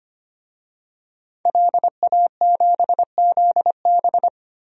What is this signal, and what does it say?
Morse code, heard as a keyed tone, about 25 wpm, 700 Hz. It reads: LA7ZB
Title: DXCC-100 Call Signs - 5 Characters - 25wpm